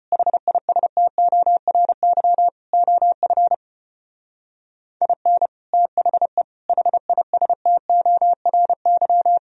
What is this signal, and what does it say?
Morse code, heard as a keyed tone, about 31 wpm, 700 Hz. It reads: HISTORY OF IN THE HISTORY
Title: Speed-Racing - Sets of 4 Words 25wpm